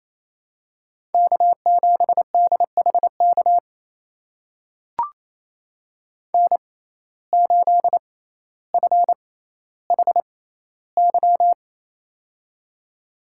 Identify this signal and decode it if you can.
Morse code, heard as a keyed tone, about 27 wpm, spaced as for 14 wpm, 700 Hz. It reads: K7DHK N8FHY
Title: USA Call Signs - 5 Characters - 6x Character Spacing - 28wpm